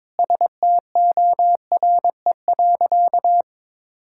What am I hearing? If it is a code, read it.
Morse code, heard as a keyed tone, about 21 wpm, 700 Hz. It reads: STORE.
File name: practice / 22wpm / Sentences with 2-Word Phrases - 22wpm